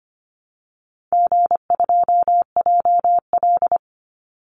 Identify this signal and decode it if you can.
Morse code, heard as a keyed tone, about 25 wpm, 700 Hz. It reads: G2JL
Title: Call Signs - 25wpm